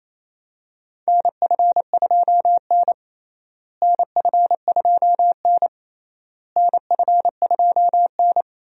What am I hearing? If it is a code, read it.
Morse code, heard as a keyed tone, about 28 wpm, 700 Hz. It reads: NF2N NF2N NF2N
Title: USA Call Signs - 4 Characters - Repeated 3x - 28wpm